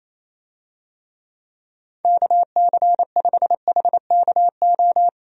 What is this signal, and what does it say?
Morse code, heard as a keyed tone, about 28 wpm, 700 Hz. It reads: KC5HKO
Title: USA Call Signs - 6 Characters - 28wpm